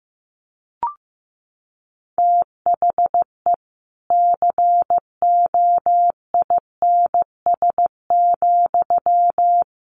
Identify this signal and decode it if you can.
Morse code, heard as a keyed tone, about 15 wpm, 700 Hz. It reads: THE COINS,
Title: A Sound Of Thunder 15wpm